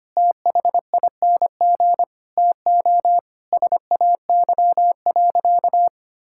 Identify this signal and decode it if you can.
Morse code, heard as a keyed tone, about 25 wpm, 700 Hz. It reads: THING TO SAY.
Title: Sentences from Top 1000 Words - Review Part 5 - 25wpm